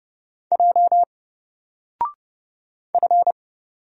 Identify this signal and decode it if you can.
Morse code, heard as a keyed tone, about 30 wpm, 700 Hz. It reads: J F
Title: Single Character (Letter:Number:Punctuation:Prosign) Alt 30wpm